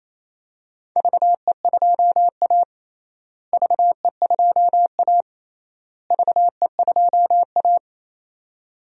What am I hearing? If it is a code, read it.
Morse code, heard as a keyed tone, about 28 wpm, 700 Hz. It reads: VE2A VE2A VE2A
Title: DXCC-100 Call Signs - 4 Characters - Repeated 3x - 28wpm